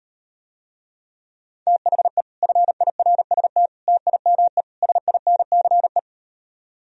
Morse code, heard as a keyed tone, about 38 wpm, 700 Hz. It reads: THE FIRST TIME SINCE